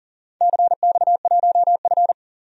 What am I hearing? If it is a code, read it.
Morse code, heard as a keyed tone, about 40 wpm, 700 Hz. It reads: CX1F
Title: DXCC-100 Call Signs - 4 Characters - 40wpm